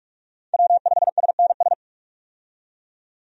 Morse code, heard as a keyed tone, about 46 wpm, 700 Hz. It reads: W5SNS
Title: USA Call Signs - 5 Characters - 45wpm